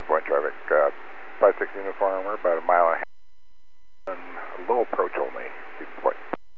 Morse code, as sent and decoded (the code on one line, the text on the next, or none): none